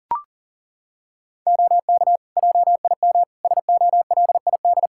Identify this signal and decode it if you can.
Morse code, heard as a keyed tone, about 40 wpm, 700 Hz. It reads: OK JIM SOLID
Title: Practice QSO - Exchange 3 - More Info - 40wpm